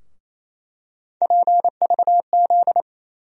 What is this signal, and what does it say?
Morse code, heard as a keyed tone, about 28 wpm, 700 Hz. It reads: PVZ
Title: USA Call Sign Postfixes - 3 Letters 28wpm